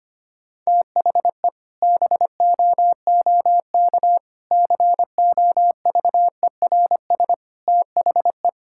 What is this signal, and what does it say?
Morse code, heard as a keyed tone, about 25 wpm, 700 Hz. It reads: THE BOOK COVERS THE
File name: practice / 25wpm / Sentences with 4-Word Phrases - 25wpm